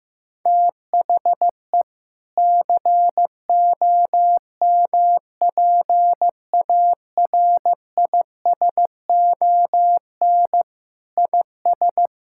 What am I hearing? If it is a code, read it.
Morse code, heard as a keyed tone, about 15 wpm, 700 Hz. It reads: THE COMPARISON IS
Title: Sentences from Top 1600 Words 15wpm